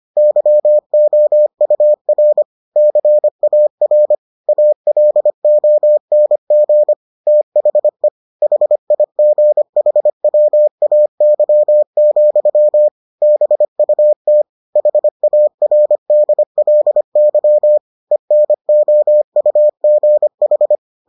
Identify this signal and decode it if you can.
Morse code, heard as a keyed tone, about 25 wpm, 600 Hz. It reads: YOUR CAR ALONG THE HIGHWAY, BUT HARDLY ENOUGH